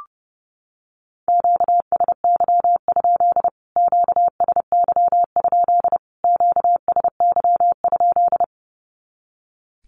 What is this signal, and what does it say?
Morse code, heard as a keyed tone, about 30 wpm, 700 Hz. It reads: QSY? QSY? QSY?
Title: Q-Codes Repeated 3x 30wpm